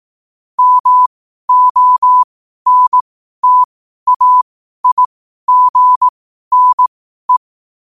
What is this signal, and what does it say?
Morse code, heard as a keyed tone, about 18 wpm, spaced as for 12 wpm, 1000 Hz. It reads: MONTAIGNE
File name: qotdfiles / 12wpm